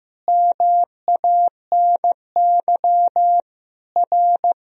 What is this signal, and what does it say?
Morse code, heard as a keyed tone, about 15 wpm, 700 Hz. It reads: MANY R
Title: Sentences from Top 100 Words and Abbreviations 15wpm